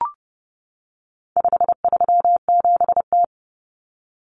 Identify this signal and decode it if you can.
Morse code, heard as a keyed tone, about 30 wpm, 700 Hz. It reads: 537T
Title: Common Cut Numbers - 4-Digit 30wpm